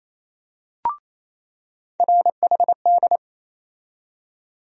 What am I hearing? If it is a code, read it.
Morse code, heard as a keyed tone, about 28 wpm, 700 Hz. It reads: RHD
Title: USA Call Sign Postfixes - 3 Letters 28wpm